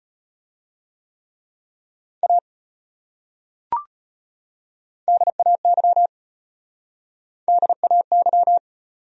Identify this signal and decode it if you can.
Morse code, heard as a keyed tone, about 35 wpm, 700 Hz. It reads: A DAY DAY